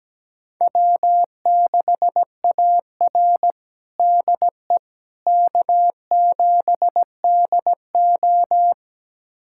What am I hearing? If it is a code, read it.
Morse code, heard as a keyed tone, about 17 wpm, 700 Hz. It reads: W6AR DE K7DO
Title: USA Callsign DE Callsign - 4 Characters - 17wpm